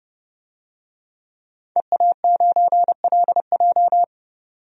Morse code, heard as a keyed tone, about 30 wpm, 700 Hz. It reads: EA9LJ
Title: DXCC-100 Call Signs - 5 Characters - 30wpm